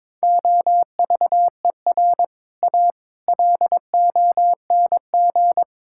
Morse code, heard as a keyed tone, about 22 wpm, 700 Hz. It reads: OVER A LONG
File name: practice / 22wpm / Sets of 6 Words - 22wpm